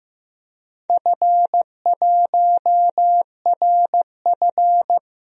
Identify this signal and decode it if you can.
Morse code, heard as a keyed tone, about 15 wpm, 700 Hz. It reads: F1RF
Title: DXCC-100 Call Signs - 4 Characters - Repeated 2x - 15wpm